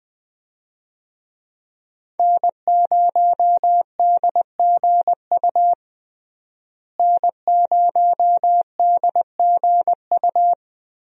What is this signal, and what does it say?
Morse code, heard as a keyed tone, about 20 wpm, 700 Hz. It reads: N0DGU N0DGU
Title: USA Call Signs - 5 Characters - Repeated 3x - 20wpm